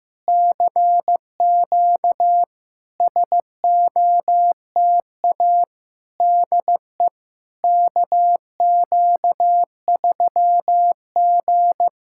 Morse code, heard as a keyed tone, about 15 wpm, 700 Hz. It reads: CQ SOTA DE KQ3G
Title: SOTA QSO Practice 15wpm